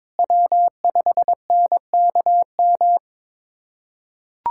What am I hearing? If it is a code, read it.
Morse code, heard as a keyed tone, about 22 wpm, 700 Hz. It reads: W5NKM
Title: USA Callsign DE Callsign 22wpm